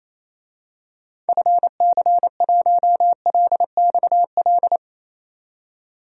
Morse code, heard as a keyed tone, about 28 wpm, 700 Hz. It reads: FC1LXL
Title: DXCC-100 Call Signs - 6 Characters - 28wpm